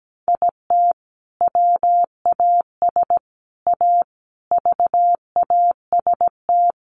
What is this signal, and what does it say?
Morse code, heard as a keyed tone, about 17 wpm, 700 Hz. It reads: IT WAS A VAST
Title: Sentences from Top 1300 Words 17wpm